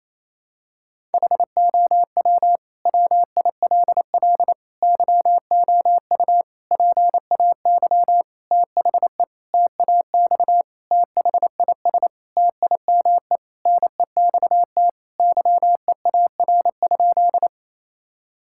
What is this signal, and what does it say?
Morse code, heard as a keyed tone, about 28 wpm, 700 Hz. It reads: HOW WILL YOU PAY THE TAX THIS TIME NEXT YEAR?
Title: Sentences from Top 1000 Words - Review Part 5 - 28wpm